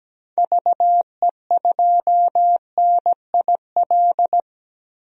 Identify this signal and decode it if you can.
Morse code, heard as a keyed tone, about 17 wpm, 700 Hz. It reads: VE2NIL
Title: DXCC-100 Call Signs - 6 Characters - 17wpm